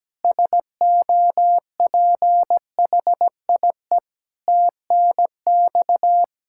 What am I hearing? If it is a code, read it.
Morse code, heard as a keyed tone, about 17 wpm, 700 Hz. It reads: SOPHIE TNX
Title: K1USN SST QSO Practice 17wpm